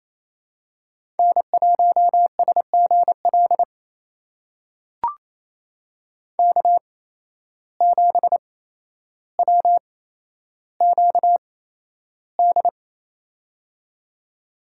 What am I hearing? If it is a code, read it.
Morse code, heard as a keyed tone, about 28 wpm, spaced as for 12 wpm, 700 Hz. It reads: N1SGL K7WQD
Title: USA Call Signs - 5 Characters - 8x Character Spacing - 28wpm